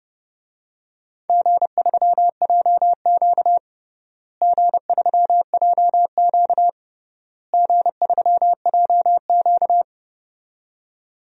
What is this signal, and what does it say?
Morse code, heard as a keyed tone, about 30 wpm, 700 Hz. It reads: G3JQ G3JQ G3JQ
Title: DXCC-100 Call Signs - 4 Characters - Repeated 3x - 30wpm